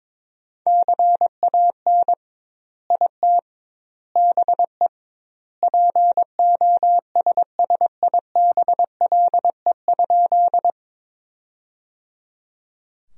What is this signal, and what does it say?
Morse code, heard as a keyed tone, about 21 wpm, 700 Hz. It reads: CAN IT BE POSSIBLE?
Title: 2x Word-Spacing - Sentences from Top 500 Words - 22wpm